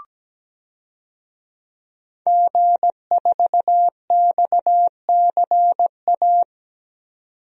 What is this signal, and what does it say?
Morse code, heard as a keyed tone, about 17 wpm, 700 Hz. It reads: G4XCA
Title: DXCC-100 Call Signs - 5 Characters - 17wpm